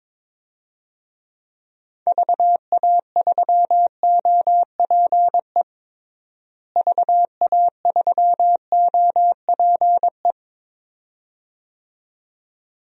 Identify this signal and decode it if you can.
Morse code, heard as a keyed tone, about 21 wpm, 700 Hz. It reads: VA3OPE VA3OPE
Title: DXCC-100 Call Signs - 6 Characters - Repeated 2x - 22wpm